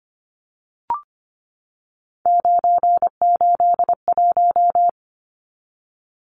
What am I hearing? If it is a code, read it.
Morse code, heard as a keyed tone, about 25 wpm, 700 Hz. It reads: 981
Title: Numbers - 3-Digit 25wpm